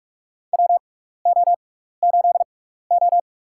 Morse code, heard as a keyed tone, about 46 wpm, spaced as for 15 wpm, 700 Hz. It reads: WO8O
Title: USA Call Signs - 4 Characters - 6x Character Spacing - 45wpm